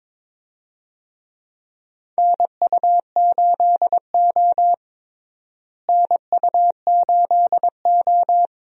Morse code, heard as a keyed tone, about 22 wpm, 700 Hz. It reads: NU8O NU8O
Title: USA Call Signs - 4 Characters - Repeated 3x - 22wpm